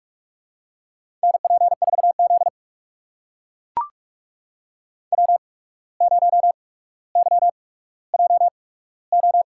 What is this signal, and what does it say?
Morse code, heard as a keyed tone, about 44 wpm, spaced as for 18 wpm, 700 Hz. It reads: NP4Z W0YJO
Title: USA Call Signs - 8x Character Spacing - 45wpm